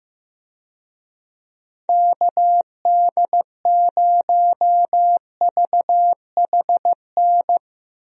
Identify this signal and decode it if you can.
Morse code, heard as a keyed tone, about 15 wpm, 700 Hz. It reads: KD0VHN
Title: USA Call Signs - 6 Characters - Repeated 2x - 15wpm